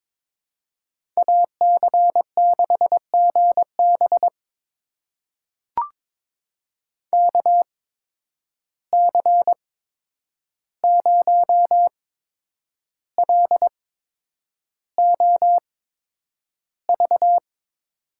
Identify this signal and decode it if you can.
Morse code, heard as a keyed tone, about 22 wpm, spaced as for 9 wpm, 700 Hz. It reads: AC6GB KC0LOV
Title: USA Call Signs - 8x Character Spacing - 22wpm